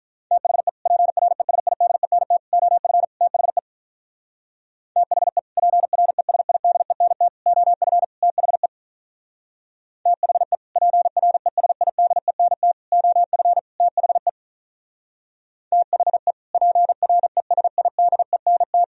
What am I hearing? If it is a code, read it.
Morse code, heard as a keyed tone, about 45 wpm, 700 Hz. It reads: THE PRESIDENT OF THE THE PRESIDENT OF THE THE PRESIDENT OF THE THE PRESIDENT